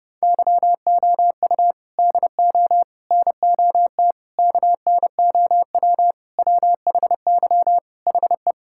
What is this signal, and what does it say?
Morse code, heard as a keyed tone, about 30 wpm, 700 Hz. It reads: YOU DO NOT KNOW WHY HE
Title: Sentences from Top 1000 Words - Review Part 6 - 30wpm